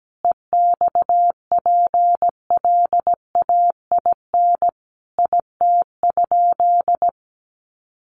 Morse code, heard as a keyed tone, about 17 wpm, 700 Hz. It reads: EXPLAIN IT?
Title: Sentences from Top 1100 Words 17wpm